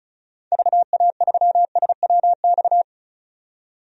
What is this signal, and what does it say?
Morse code, heard as a keyed tone, about 36 wpm, 700 Hz. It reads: VA3SWX
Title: DXCC-100 Call Signs - 6 Characters - 35wpm